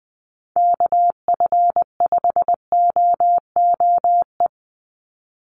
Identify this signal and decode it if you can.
Morse code, heard as a keyed tone, about 20 wpm, 700 Hz. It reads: KF5OOE
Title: USA Call Signs 20wpm